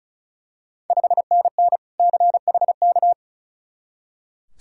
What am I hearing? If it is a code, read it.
Morse code, heard as a keyed tone, about 36 wpm, 700 Hz. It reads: 5NN CHK
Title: Ontario QSO Party Exchange 35wpm